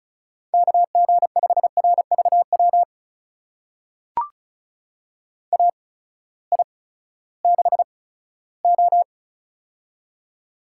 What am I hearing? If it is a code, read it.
Morse code, heard as a keyed tone, about 36 wpm, spaced as for 18 wpm, 700 Hz. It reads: KG5RVW AI6O
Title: USA Call Signs - 8x Character Spacing - 35wpm